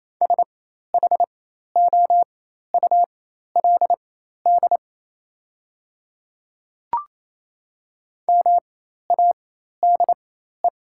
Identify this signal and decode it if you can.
Morse code, heard as a keyed tone, about 28 wpm, spaced as for 13 wpm, 690 Hz. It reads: SHOULD MADE
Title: 4x Character Spacing - Top 300 Words - 28wpm